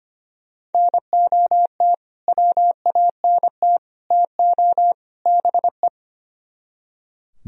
Morse code, heard as a keyed tone, about 31 wpm, 690 Hz. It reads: NOT WANT TO BE
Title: Speed-Racing - Sets of 4 Words 25wpm